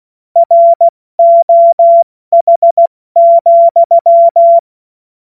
Morse code, heard as a keyed tone, about 16 wpm, 680 Hz. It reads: ROH,
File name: Morsekurs / Witze / Witz 01. Februar 2022 70 F80